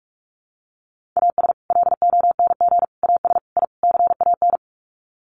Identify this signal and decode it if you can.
Morse code, heard as a keyed tone, about 46 wpm, 700 Hz. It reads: AS LONG AS I CAN
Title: Sets of 5 Words Encore2 45wpm